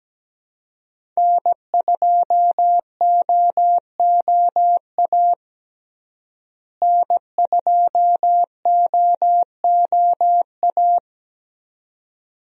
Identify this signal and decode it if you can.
Morse code, heard as a keyed tone, about 17 wpm, 700 Hz. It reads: N2OOA N2OOA
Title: USA Call Signs - 5 Characters - Repeated 2x - 17wpm